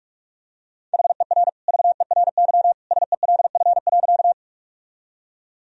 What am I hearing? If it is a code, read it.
Morse code, heard as a keyed tone, about 44 wpm, 700 Hz. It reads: HER VERY SELF.
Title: Sentences from Top 1000 Words - Review Part 1 - 45wpm